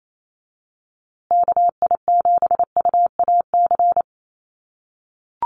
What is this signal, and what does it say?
Morse code, heard as a keyed tone, about 28 wpm, 700 Hz. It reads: KI7UAC
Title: USA Call Signs Repeated 3x 28wpm